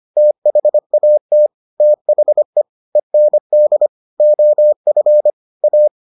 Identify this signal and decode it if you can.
Morse code, heard as a keyed tone, about 25 wpm, 600 Hz. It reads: THAT THE END OF A